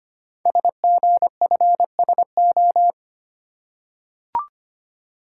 Morse code, heard as a keyed tone, about 25 wpm, 700 Hz. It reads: SGFSO
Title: Code Groups - 5 Letters - 25wpm